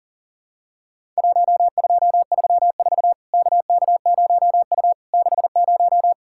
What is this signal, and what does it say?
Morse code, heard as a keyed tone, about 42 wpm, 700 Hz. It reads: 1234 KK0U 60
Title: Worked All Europe DX Contest - QTC Exchange - 40wpm